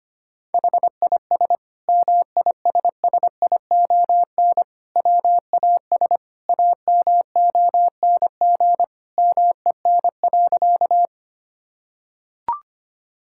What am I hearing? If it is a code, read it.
Morse code, heard as a keyed tone, about 25 wpm, 700 Hz. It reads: HIS MISSION WAS AMONG MEN.
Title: Sentences from Top 1600 Words 25wpm